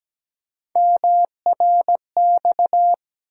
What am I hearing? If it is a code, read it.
Morse code, heard as a keyed tone, about 17 wpm, 700 Hz. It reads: MRX
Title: USA Call Sign Postfixes - 3 Letters 17wpm